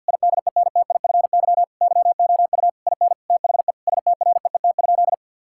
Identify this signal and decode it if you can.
Morse code, heard as a keyed tone, about 51 wpm, 700 Hz. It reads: IDENTIFY YOU IN THE STREET?
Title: Sentences from Top 1000 Words - Review Part 6 - 50wpm